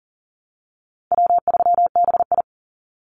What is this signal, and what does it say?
Morse code, heard as a keyed tone, about 40 wpm, 700 Hz. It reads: W3BI
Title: Easy Call Signs Repeated 2x - 40wpm